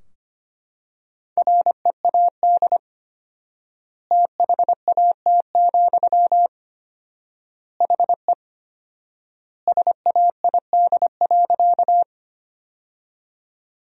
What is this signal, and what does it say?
Morse code, heard as a keyed tone, about 25 wpm, 700 Hz. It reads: READ THAT, HE SAID.
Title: 4x Word-Spacing - Sentences from Top 200 Words - 25wpm